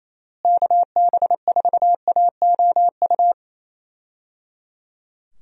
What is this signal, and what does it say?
Morse code, heard as a keyed tone, about 28 wpm, 700 Hz. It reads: KB4AOU
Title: USA Call Signs - 6 Characters - Repeated 3x - 28wpm